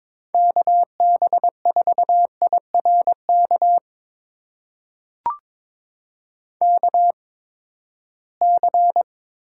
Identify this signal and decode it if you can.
Morse code, heard as a keyed tone, about 22 wpm, spaced as for 15 wpm, 700 Hz. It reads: KB4IRK KC
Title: USA Call Signs - 6 Characters - 8x Character Spacing - 22wpm